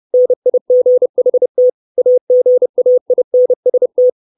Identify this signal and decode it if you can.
Morse code, heard as a keyed tone, about 30 wpm, 500 Hz. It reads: NIGHT AGAINST
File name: 632 words / 632 Words 30WPM